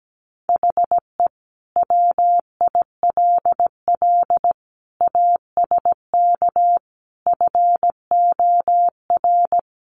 Morse code, heard as a keyed tone, about 17 wpm, 700 Hz. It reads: HE WILL ASK FOR
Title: Sentences from Top 200 Words 17wpm